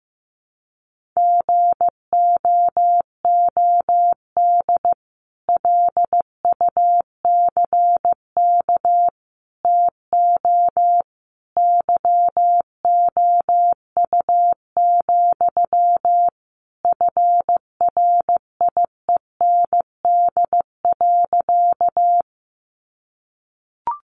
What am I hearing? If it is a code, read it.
Morse code, heard as a keyed tone, about 14 wpm, 700 Hz. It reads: GOOD LUCK TO YOU, FRIEND.
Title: Sentences from Top 1500 Words 15wpm